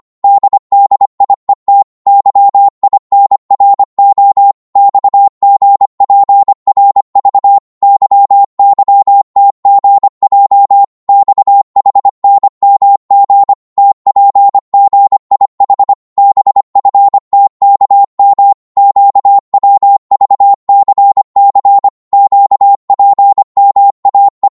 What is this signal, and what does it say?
Morse code, heard as a keyed tone, about 25 wpm, 800 Hz. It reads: DDIET YINRO XGPRV YYTGJ XHNMG TPGIH BFTKM QWVCC QPMAE
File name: Morse / ltr-25wpm